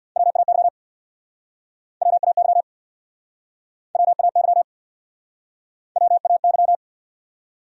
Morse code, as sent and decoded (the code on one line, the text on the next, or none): .-- .- -.-- / .-- .- -.-- / .-- .- -.-- / .-- .- -.--
WAY WAY WAY WAY